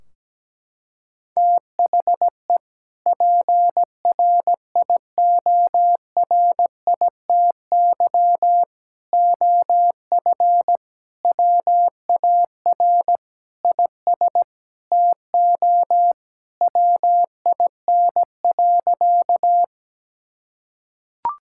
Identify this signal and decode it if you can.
Morse code, heard as a keyed tone, about 17 wpm, 700 Hz. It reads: THE PRIORITY OF WAR IS TO WIN.
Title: Sentences from Top 1600 Words 17wpm